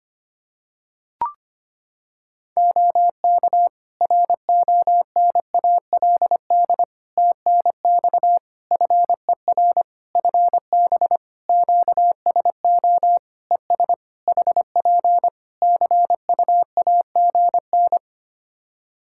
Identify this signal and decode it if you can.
Morse code, heard as a keyed tone, about 25 wpm, 700 Hz. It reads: OK RONALD TNX FER FB QSO ES HP CUAGN
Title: Practice QSO - Exchange 4 - Ending - 25wpm